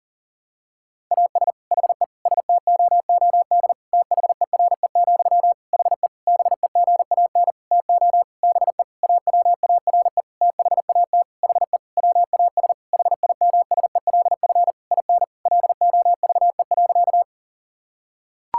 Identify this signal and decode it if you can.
Morse code, heard as a keyed tone, about 37 wpm, 700 Hz. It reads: AS HE STOOD THERE, HE BEGAN TO BE AWARE THAT HE WAS HIMSELF IN LOVE.
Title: Sentences from Top 1600 Words 40wpm